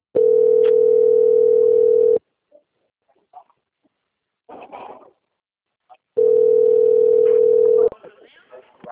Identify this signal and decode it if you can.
Ringback tone (North America), 440 + 480 Hz, with speech.
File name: out- - - - - .4143